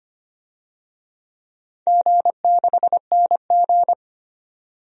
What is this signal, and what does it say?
Morse code, heard as a keyed tone, about 25 wpm, 690 Hz. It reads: G6NG
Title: DXCC-100 Call Signs - 4 Characters - 25wpm